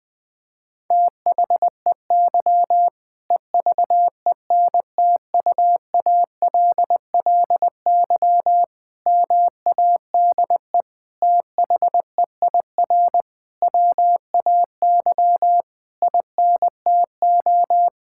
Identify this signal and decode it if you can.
Morse code, heard as a keyed tone, about 21 wpm, 700 Hz. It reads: THEY EVENTUALLY MADE THEIR WAY INTO
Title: Sentences from Top 1000 Words - Review Part 6 - 20wpm